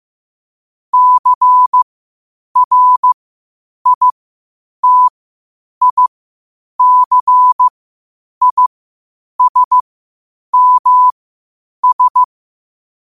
Morse code, heard as a keyed tone, about 15 wpm, spaced as for 8 wpm, 1000 Hz. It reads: CRITICISMS
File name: qotdfiles / 8wpm